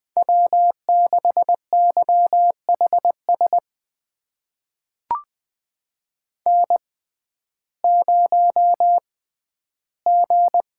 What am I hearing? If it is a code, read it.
Morse code, heard as a keyed tone, about 21 wpm, spaced as for 12 wpm, 700 Hz. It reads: W6YHS N0G
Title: USA Call Signs - 5 Characters - 6x Character Spacing - 20wpm